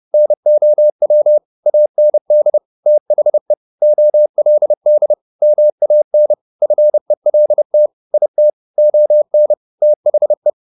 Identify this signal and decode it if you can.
Morse code, heard as a keyed tone, about 30 wpm, 600 Hz. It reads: NOW AND THE OLD MAN FELT IT ON THE
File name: the old man and the sea / the old man and the sea 1 30wpm